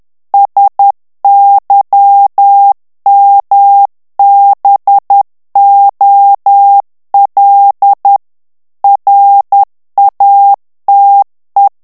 Morse code, heard as a keyed tone, about 10 wpm, 800 Hz. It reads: SYMBOL RATE
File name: ANGLAIS texte clair 3 vitesse 10 mots minute